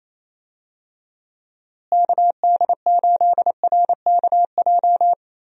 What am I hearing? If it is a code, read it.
Morse code, heard as a keyed tone, about 27 wpm, 700 Hz. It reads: KD8RKJ
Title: USA Call Signs - 6 Characters - Repeated 3x - 28wpm